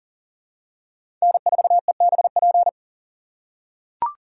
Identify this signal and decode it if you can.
Morse code, heard as a keyed tone, about 40 wpm, 700 Hz. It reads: N4EBP
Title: USA Call Signs - 6x Character Spacing - 40wpm